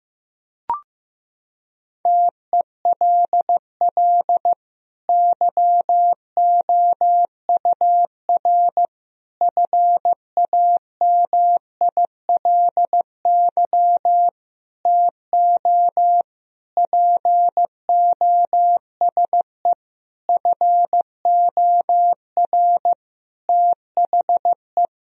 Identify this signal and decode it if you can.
Morse code, heard as a keyed tone, about 15 wpm, 700 Hz. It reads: TELL YOUR FAMILY TO POSE FOR THE
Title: Sentences from Top 1000 Words - Review Part 2 - 15wpm